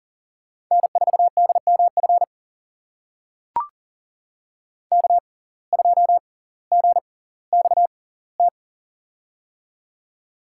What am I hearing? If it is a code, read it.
Morse code, heard as a keyed tone, about 40 wpm, spaced as for 20 wpm, 700 Hz. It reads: N4DMF K2GXT
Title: USA Call Signs - 6x Character Spacing - 40wpm